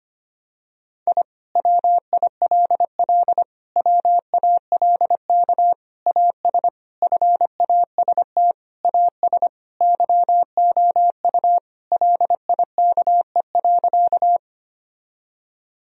Morse code, heard as a keyed tone, about 25 wpm, 700 Hz. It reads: I WILL WALK AS FAST AS YOU LIKE.
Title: Sentences from Top 1000 Words - Review Part 2 - 25wpm